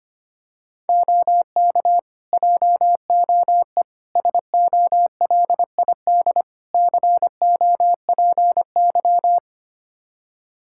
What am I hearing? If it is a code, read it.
Morse code, heard as a keyed tone, about 25 wpm, 700 Hz. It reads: OK JOE SOLID COPY
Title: K1USN SST QSO Practice 25wpm